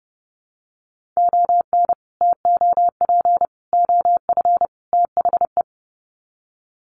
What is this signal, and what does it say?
Morse code, heard as a keyed tone, about 30 wpm, 700 Hz. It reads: ON TOP OF THE
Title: Sets of 4 Words 30wpm